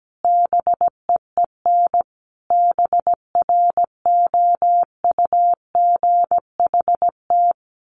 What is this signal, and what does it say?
Morse code, heard as a keyed tone, about 17 wpm, 700 Hz. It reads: BEEN BROUGHT
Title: Sentences from Top 1500 Words 17wpm